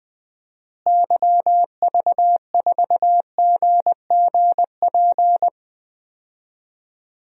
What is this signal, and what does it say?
Morse code, heard as a keyed tone, about 20 wpm, 700 Hz. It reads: YV4GGP
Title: DXCC-100 Call Signs - 6 Characters - Repeated 2x - 20wpm